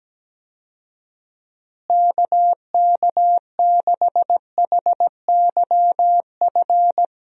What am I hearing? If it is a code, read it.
Morse code, heard as a keyed tone, about 17 wpm, 700 Hz. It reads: KK6HYF
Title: USA Call Signs - 6 Characters - Repeated 3x - 17wpm